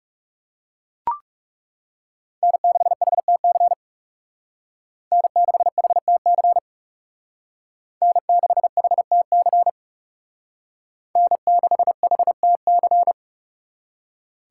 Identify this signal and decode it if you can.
Morse code, heard as a keyed tone, about 38 wpm, 700 Hz. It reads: N6HTC N6HTC N6HTC N6HTC